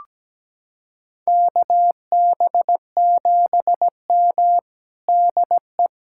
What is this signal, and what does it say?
Morse code, heard as a keyed tone, about 17 wpm, 700 Hz. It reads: KB7M DE
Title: USA Callsign DE Callsign - 4 Characters - 17wpm